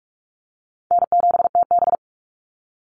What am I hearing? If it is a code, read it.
Morse code, heard as a keyed tone, about 44 wpm, 700 Hz. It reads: N7TB
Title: Easy Call Signs Repeated 3x - 45wpm